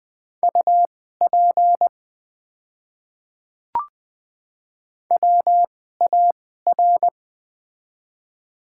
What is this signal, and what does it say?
Morse code, heard as a keyed tone, about 21 wpm, spaced as for 15 wpm, 700 Hz. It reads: UP WAR
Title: 2x Character Spacing - Top 500 Words - 20wpm